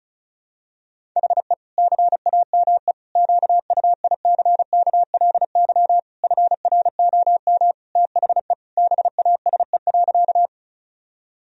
Morse code, heard as a keyed tone, about 36 wpm, 700 Hz. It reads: HE CAME QUICKLY FROM THE BASE.
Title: Sentences from Top 1000 Words - Review Part 1 - 35wpm